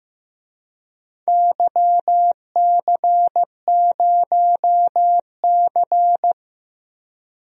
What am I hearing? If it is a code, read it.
Morse code, heard as a keyed tone, about 15 wpm, 700 Hz. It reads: YC0C
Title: DXCC-100 Call Signs - 4 Characters - Repeated 2x - 15wpm